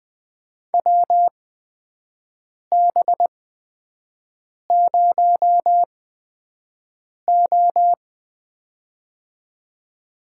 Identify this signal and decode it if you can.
Morse code, heard as a keyed tone, about 21 wpm, spaced as for 5 wpm, 700 Hz. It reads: WB0O
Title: USA Call Signs - 8x Character Spacing - 20wpm